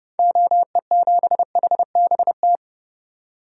Morse code, heard as a keyed tone, about 32 wpm, 700 Hz. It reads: OE7HBT
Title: DXCC-100 Call Signs - 6 Characters - 30wpm